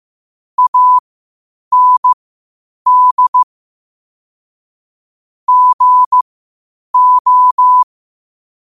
Morse code, heard as a keyed tone, about 15 wpm, spaced as for 8 wpm, 1000 Hz. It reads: AND GO
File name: 8wpm